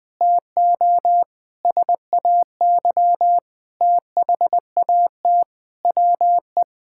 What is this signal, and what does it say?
Morse code, heard as a keyed tone, about 20 wpm, 700 Hz. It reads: TO SAY THAT WE
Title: Sentences with 7-Word Phrases - 20wpm